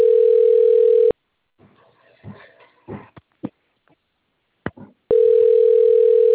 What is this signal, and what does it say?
Ringback tone (North America), 440 + 480 Hz.
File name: out- - - - - .29497